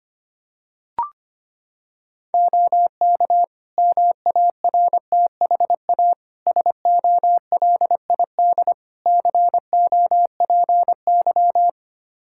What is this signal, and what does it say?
Morse code, heard as a keyed tone, about 25 wpm, 700 Hz. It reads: OK MARTHA SOLID COPY
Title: Practice QSO - Exchange 3 - More Info - 25wpm